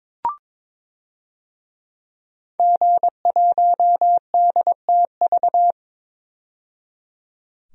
Morse code, heard as a keyed tone, about 22 wpm, 700 Hz. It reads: G1DTV
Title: DXCC-100 Call Signs - 5 Characters - 22wpm